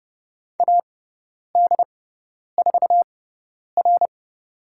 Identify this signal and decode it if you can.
Morse code, heard as a keyed tone, about 30 wpm, spaced as for 10 wpm, 700 Hz. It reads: AD4R
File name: Lesson 17g - Letter u - Call Signs 30wpm